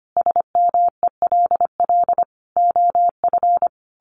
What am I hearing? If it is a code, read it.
Morse code, heard as a keyed tone, about 25 wpm, 700 Hz. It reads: SMELL OF
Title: A Sound Of Thunder 25wpm